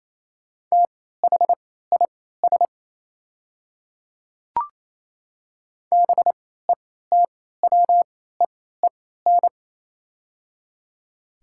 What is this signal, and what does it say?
Morse code, heard as a keyed tone, about 28 wpm, spaced as for 16 wpm, 700 Hz. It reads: THIS BETWEEN
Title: 3x Character Spacing - Top 300 Words - 28wpm